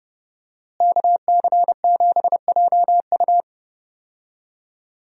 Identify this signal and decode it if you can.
Morse code, heard as a keyed tone, about 30 wpm, 700 Hz. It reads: KC7JU
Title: USA Call Signs - 5 Characters - Repeated 2x - 30wpm